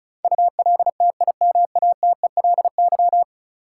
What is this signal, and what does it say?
Morse code, heard as a keyed tone, about 34 wpm, 700 Hz. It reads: ULTIMATELY
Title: Sentences from Top 1200 Words - New Word Drill - 35wpm